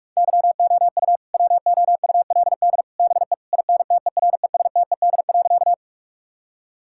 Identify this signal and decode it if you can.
Morse code, heard as a keyed tone, about 46 wpm, 700 Hz. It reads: YOU WOULD BE INTERESTED.
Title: Sentences from Top 1000 Words - Review Part 6 - 45wpm